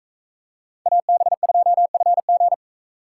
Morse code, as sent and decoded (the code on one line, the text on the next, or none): .- -... ..--- ..-. --.
AB2FG